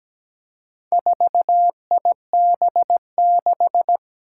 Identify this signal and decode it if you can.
Morse code, heard as a keyed tone, about 17 wpm, 700 Hz. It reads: 4IB6
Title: Four Challenging Letter-Number Combinations 17wpm